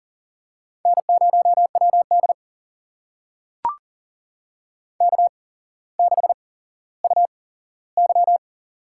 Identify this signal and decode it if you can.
Morse code, heard as a keyed tone, about 40 wpm, spaced as for 17 wpm, 700 Hz. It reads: N0WD K6UY
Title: USA Call Signs - 4 Characters - 8x Character Spacing - 40wpm